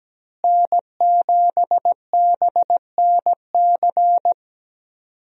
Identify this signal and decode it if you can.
Morse code, heard as a keyed tone, about 17 wpm, 700 Hz. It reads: N7BNC
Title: USA Call Signs - 5 Characters - Repeated 2x - 17wpm